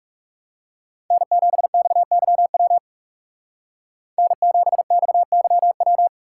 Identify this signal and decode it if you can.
Morse code, heard as a keyed tone, about 39 wpm, 700 Hz. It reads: N7XYW N7XYW